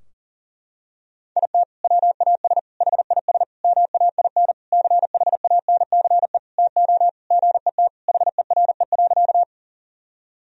Morse code, heard as a keyed tone, about 40 wpm, 700 Hz. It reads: IT WAS HIS MAIN CHANCE TO GET HERE.
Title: Sentences from Top 1000 Words - Review Part 2 - 40wpm